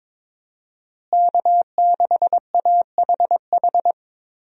Morse code, heard as a keyed tone, about 22 wpm, 700 Hz. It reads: K6AHH
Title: USA Call Signs - 5 Characters - Repeated 3x - 22wpm